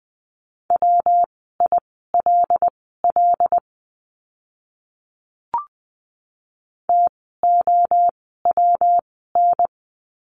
Morse code, heard as a keyed tone, about 20 wpm, spaced as for 14 wpm, 700 Hz. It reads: WILL TOWN
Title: 115 Lesson 19f - Letter w - Words 20wpm